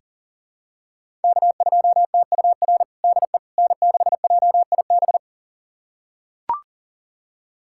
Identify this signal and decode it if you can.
Morse code, heard as a keyed tone, about 40 wpm, 700 Hz. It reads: K2TUR DE N6JIB
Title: USA Callsign DE Callsign - 5 Characters - 40wpm